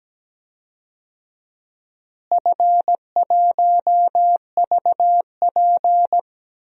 Morse code, heard as a keyed tone, about 17 wpm, 700 Hz. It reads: F1VP
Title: DXCC-100 Call Signs - 4 Characters - Repeated 3x - 17wpm